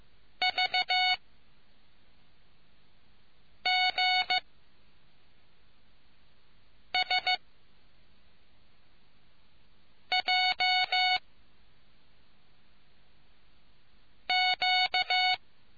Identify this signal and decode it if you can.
Morse code, heard as a keyed tone, about 15 wpm, spaced as for 3 wpm, 740 Hz. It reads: VGSJQ